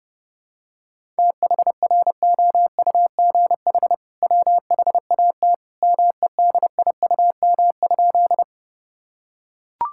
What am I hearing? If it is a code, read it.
Morse code, heard as a keyed tone, about 30 wpm, 700 Hz. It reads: THROUGH WHAT MEDIUM?
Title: Sentences from Top 1000 Words - Review Part 6 - 30wpm